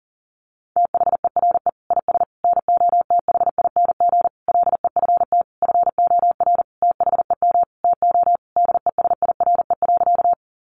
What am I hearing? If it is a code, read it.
Morse code, heard as a keyed tone, about 40 wpm, 700 Hz. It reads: THERE IS NOTHING LEFT FOR THEM TO DESIRE.
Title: Sentences from Top 1100 Words 40wpm